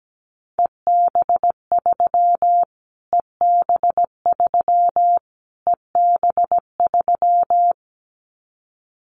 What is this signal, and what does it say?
Morse code, heard as a keyed tone, about 17 wpm, 700 Hz. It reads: EB3 EB3 EB3
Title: Callsign Prefixes for Top 25 DXCC - Repeated 3x - 17wpm